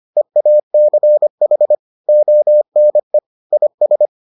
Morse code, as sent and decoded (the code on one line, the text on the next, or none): . .- -.-. .... / --- -. / .. ...
EACH ONE IS